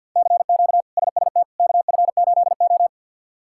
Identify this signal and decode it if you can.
Morse code, heard as a keyed tone, about 48 wpm, 700 Hz. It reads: CQ SST KF8O
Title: K1USN SST QSO Practice - 50wpm